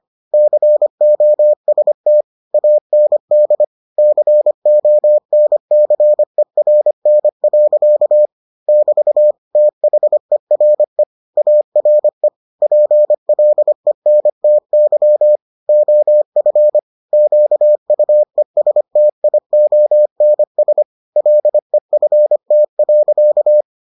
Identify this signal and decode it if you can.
Morse code, heard as a keyed tone, about 25 wpm, 600 Hz. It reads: COST AND CONCERN. <BT> THERE ARE PLENTY OF QUESTIONS LEFT.